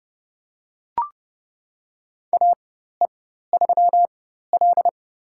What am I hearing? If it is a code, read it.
Morse code, heard as a keyed tone, about 32 wpm, spaced as for 14 wpm, 700 Hz. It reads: AE3L
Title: USA Call Signs - 4x Character Spacing - 30wpm